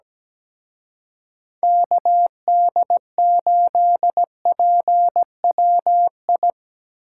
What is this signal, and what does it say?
Morse code, heard as a keyed tone, about 20 wpm, 700 Hz. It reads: KD8PWI
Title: Speed Racing - USA 2x3 Call Signs 17wpm